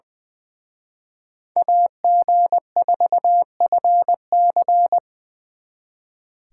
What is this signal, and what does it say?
Morse code, heard as a keyed tone, about 24 wpm, 700 Hz. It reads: AG4FC
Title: Speed Racing - USA 2x2 Call Signs 20wpm